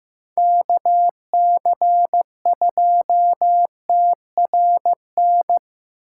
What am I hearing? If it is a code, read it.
Morse code, heard as a keyed tone, about 15 wpm, 700 Hz. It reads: KC2TRN
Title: USA Call Signs - 6 Characters - 6x Character Spacing - 15wpm